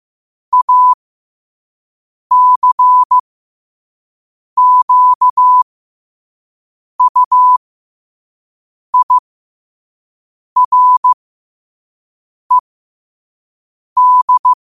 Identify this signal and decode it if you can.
Morse code, heard as a keyed tone, about 15 wpm, spaced as for 5 wpm, 1000 Hz. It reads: ACQUIRED
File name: qotdfiles / 5wpm